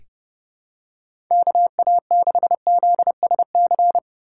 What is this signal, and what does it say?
Morse code, heard as a keyed tone, about 32 wpm, 700 Hz. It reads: KA6ZSC
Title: USA Call Signs - 6 Characters - 3x Character Spacing - 30wpm